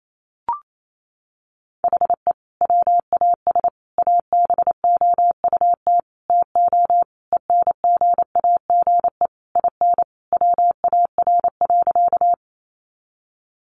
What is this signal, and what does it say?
Morse code, heard as a keyed tone, about 28 wpm, 700 Hz. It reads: HE WAS ABOUT TO ENGAGE IN WAR.
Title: Sentences from Top 1500 Words 28wpm